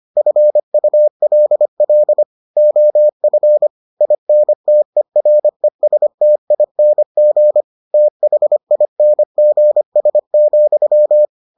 Morse code, heard as a keyed tone, about 25 wpm, 600 Hz. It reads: FULL OF INTERESTING THINGS,